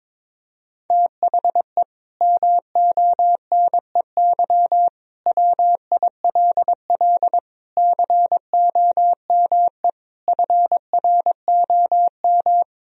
Morse code, heard as a keyed tone, about 22 wpm, 700 Hz. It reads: THE MONEY WILL COME FROM
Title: Sentences from Top 1000 Words - Review Part 4 - 22wpm